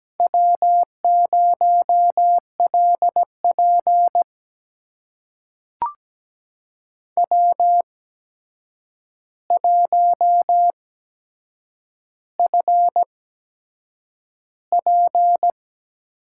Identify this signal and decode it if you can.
Morse code, heard as a keyed tone, about 17 wpm, spaced as for 7 wpm, 700 Hz. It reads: W0LP W1FP
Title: USA Call Signs - 4 Characters - 8x Character Spacing - 17wpm